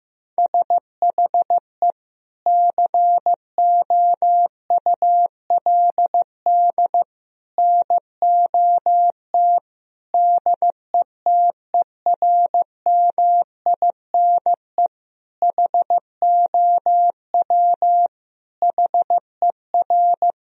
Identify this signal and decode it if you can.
Morse code, heard as a keyed tone, about 16 wpm, 700 Hz. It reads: SHE COULD NOT DETERMINE HOW HER
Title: Sentences from Top 1000 Words - Review Part 4 - 15wpm